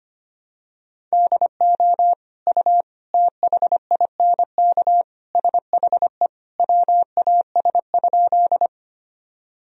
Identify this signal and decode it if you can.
Morse code, heard as a keyed tone, about 25 wpm, 700 Hz. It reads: DO U THINK SHE WAS?
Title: Sentences from Top 200 Words and Abbreviations 25wpm